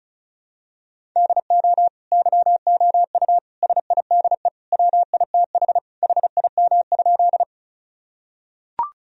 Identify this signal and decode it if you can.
Morse code, heard as a keyed tone, about 34 wpm, 700 Hz. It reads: DO YOU SIDE WITH HIM?
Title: Sentences from Top 1000 Words - Review Part 1 - 35wpm